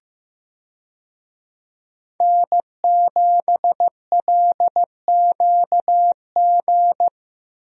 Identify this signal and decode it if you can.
Morse code, heard as a keyed tone, about 15 wpm, 700 Hz. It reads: N7LQG
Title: USA Call Signs - 5 Characters - Repeated 3x - 15wpm